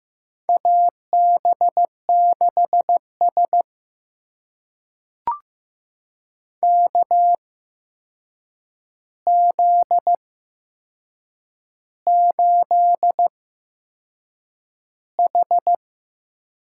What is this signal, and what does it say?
Morse code, heard as a keyed tone, about 15 wpm, spaced as for 6 wpm, 700 Hz. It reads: AB6S KZ8H
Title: USA Call Signs - 4 Characters - 8x Character Spacing - 15wpm